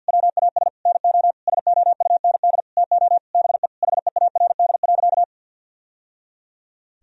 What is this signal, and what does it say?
Morse code, heard as a keyed tone, about 45 wpm, 700 Hz. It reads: WAS NO SOUND TO BE HEARD.
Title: Sentences from Top 1000 Words - Review Part 1 - 50wpm